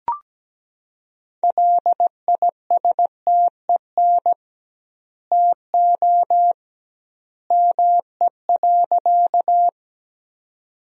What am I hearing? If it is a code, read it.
Morse code, heard as a keyed tone, about 17 wpm, 700 Hz. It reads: LISTEN TO ME.
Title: 2x Word-Spacing - Sentences from Top 400 Words - 17wpm